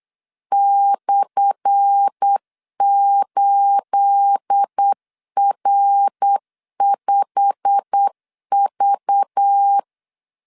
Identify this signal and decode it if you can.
Morse code, heard as a keyed tone, about 8 wpm, 800 Hz. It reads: /8R5V
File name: LEZIONI / 10 SIGNS / 10A SLASH 8wpm 3452